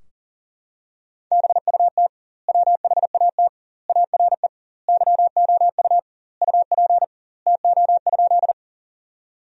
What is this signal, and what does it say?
Morse code, heard as a keyed tone, about 40 wpm, 700 Hz. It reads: BUT WHAT ARE YOU UP TO?
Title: 2x Word-Spacing - Sentences from Top 100 Words - 40wpm